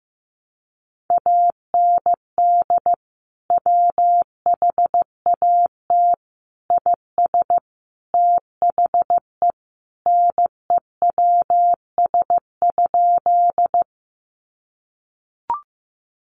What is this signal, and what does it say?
Morse code, heard as a keyed tone, about 15 wpm, 700 Hz. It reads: AND WHAT IS THE NEWS?
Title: Sentences from Top 800 Words 15wpm